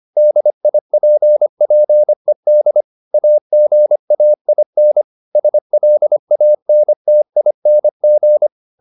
Morse code, heard as a keyed tone, about 25 wpm, 600 Hz. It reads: DIPPED AGAIN SLANTING
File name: the old man and the sea part 1 25wpm